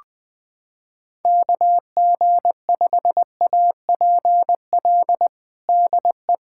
Morse code, heard as a keyed tone, about 20 wpm, 700 Hz. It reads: KG5APL DE
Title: USA Callsign DE Callsign 20wpm